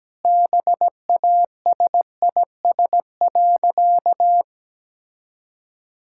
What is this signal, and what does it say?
Morse code, heard as a keyed tone, about 17 wpm, 700 Hz. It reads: BASIS.
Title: Sentences from Top 1000 Words - Review Part 5 - 17wpm